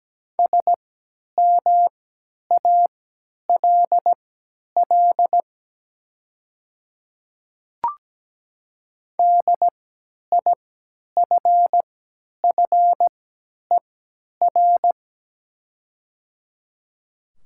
Morse code, heard as a keyed tone, about 17 wpm, spaced as for 10 wpm, 700 Hz. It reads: SMALL DIFFER
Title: 3x Character Spacing - Top 300 Words - 17wpm